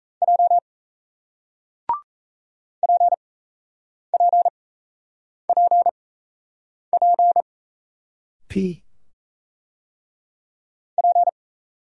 Morse code, heard as a keyed tone, about 38 wpm, spaced as for 8 wpm, 700 Hz. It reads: J PPPP P